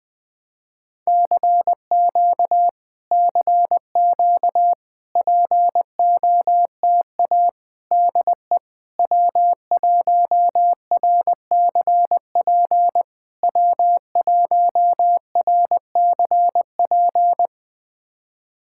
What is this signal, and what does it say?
Morse code, heard as a keyed tone, about 19 wpm, 700 Hz. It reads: CQ CQ POTA DE W1RCP W1RCP
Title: POTA QSO Practice - 20wpm